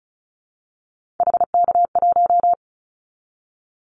Morse code, heard as a keyed tone, about 34 wpm, 700 Hz. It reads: HK1
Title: Callsign Prefixes for Top 100 DXCC - 35wpm